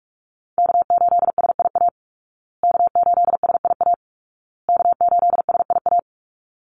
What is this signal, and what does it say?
Morse code, heard as a keyed tone, about 44 wpm, 700 Hz. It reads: K8SIA K8SIA K8SIA
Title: Call Signs Repeated 2x - 45wpm